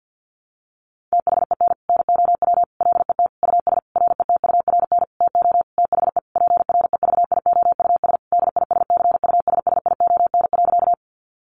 Morse code, heard as a keyed tone, about 50 wpm, 700 Hz. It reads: THEN NOW LET US RETURN TO THE PREVIOUS DISCUSSION.
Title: Sentences from Top 1500 Words 50wpm